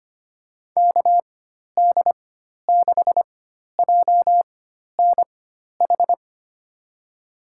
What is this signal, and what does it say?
Morse code, heard as a keyed tone, about 25 wpm, spaced as for 12 wpm, 700 Hz. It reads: KD6JNH
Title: USA Call Signs - 6 Characters - 4x Character Spacing - 25wpm